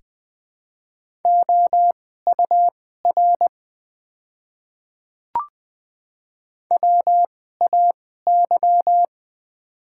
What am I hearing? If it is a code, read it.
Morse code, heard as a keyed tone, about 20 wpm, spaced as for 14 wpm, 690 Hz. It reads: OUR WAY
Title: 2x Character Spacing - Top 300 Words - 20wpm